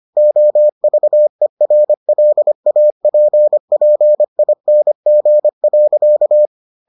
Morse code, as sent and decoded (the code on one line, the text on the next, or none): --- ...- . .-. .-.. .- .--. .--. .. -. --. .-.-.-
OVERLAPPING.